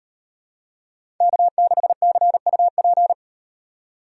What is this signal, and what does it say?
Morse code, heard as a keyed tone, about 38 wpm, 700 Hz. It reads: K6CUP